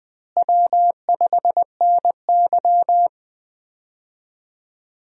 Morse code, heard as a keyed tone, about 20 wpm, 700 Hz. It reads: W5NY
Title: USA Call Signs - 4 Characters - Repeated 3x - 20wpm